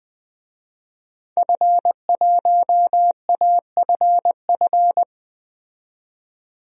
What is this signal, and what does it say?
Morse code, heard as a keyed tone, about 20 wpm, 700 Hz. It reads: F1AFF
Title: DXCC-100 Call Signs - 5 Characters - 20wpm